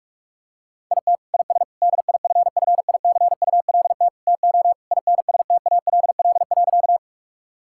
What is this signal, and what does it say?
Morse code, heard as a keyed tone, about 41 wpm, 700 Hz. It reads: IT IS DIFFICULT TO INSTALL.
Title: Sentences from Top 1600 Words 45wpm